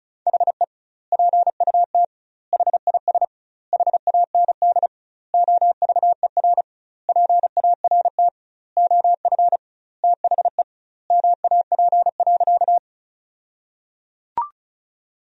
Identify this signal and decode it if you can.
Morse code, heard as a keyed tone, about 34 wpm, 700 Hz. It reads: HE PUT HIS HAND OVER PART OF THE MAP.
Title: 2x Word-Spacing - Sentences from Top 400 Words - 35wpm